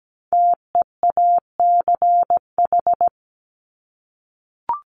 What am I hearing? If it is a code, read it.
Morse code, heard as a keyed tone, about 17 wpm, 700 Hz. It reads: TEACH
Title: Top 500 Words Repeated 17wpm